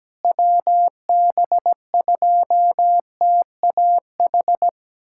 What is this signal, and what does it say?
Morse code, heard as a keyed tone, about 17 wpm, 700 Hz. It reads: WB2TAH
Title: USA Callsign DE Callsign - 6 Characters - 17wpm